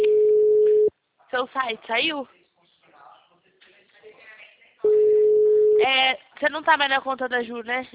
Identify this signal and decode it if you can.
Ringback tone (Europe, 425 Hz), with speech.